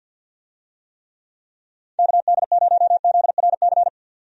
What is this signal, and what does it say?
Morse code, heard as a keyed tone, about 51 wpm, 700 Hz. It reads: KD0ZRC
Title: USA Call Signs - 6 Characters - 50wpm